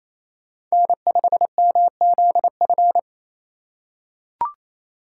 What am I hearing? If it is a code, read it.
Morse code, heard as a keyed tone, about 27 wpm, 700 Hz. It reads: N5MZF
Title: USA Call Signs - 5 Characters - 2x Character Spacing - 28wpm